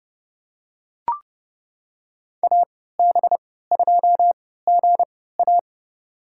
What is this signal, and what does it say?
Morse code, heard as a keyed tone, about 30 wpm, spaced as for 17 wpm, 700 Hz. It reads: AB2GA
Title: USA Call Signs - 5 Characters - 3x Character Spacing - 30wpm